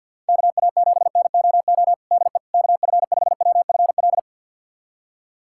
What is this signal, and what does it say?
Morse code, heard as a keyed tone, about 52 wpm, 700 Hz. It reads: KA7NOO DE KF5WFL
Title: USA Callsign DE Callsign 50wpm